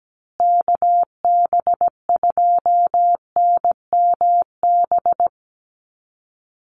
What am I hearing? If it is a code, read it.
Morse code, heard as a keyed tone, about 17 wpm, 700 Hz. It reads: KB2NMB
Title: USA Call Signs Repeated 2x 17wpm